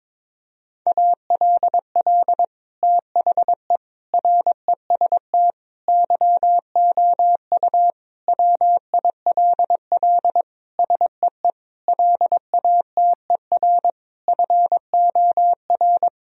Morse code, heard as a keyed tone, about 22 wpm, 700 Hz. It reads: ALL THE REST YOU WILL SEE LATER FOR
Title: Sentences from Top 1000 Words - Review Part 4 - 22wpm